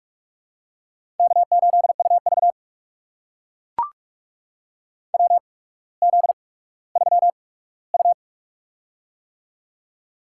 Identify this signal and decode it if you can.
Morse code, heard as a keyed tone, about 44 wpm, spaced as for 19 wpm, 690 Hz. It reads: K8UV WZ3U
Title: USA Call Signs - 4 Characters - 8x Character Spacing - 45wpm